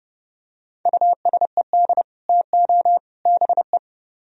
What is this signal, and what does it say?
Morse code, heard as a keyed tone, about 30 wpm, 700 Hz. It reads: USED TO BE